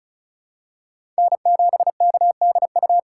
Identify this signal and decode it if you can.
Morse code, heard as a keyed tone, about 34 wpm, 700 Hz. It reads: N7KDU
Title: USA Call Signs - 5 Characters - 4x Character Spacing - 35wpm